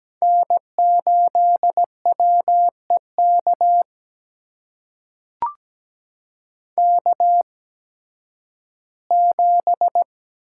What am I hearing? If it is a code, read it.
Morse code, heard as a keyed tone, about 17 wpm, spaced as for 11 wpm, 700 Hz. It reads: N8WEK K7
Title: USA Call Signs - 5 Characters - 8x Character Spacing - 17wpm